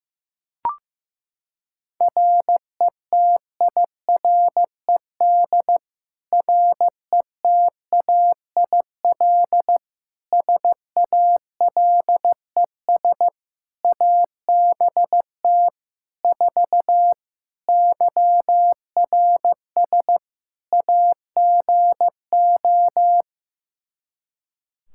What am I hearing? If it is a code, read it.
Morse code, heard as a keyed tone, about 15 wpm, 700 Hz. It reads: RETIRED RETAIL SALES ABT 4 YRS AGO
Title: Practice QSO - Exchange 3 - More Info - 15wpm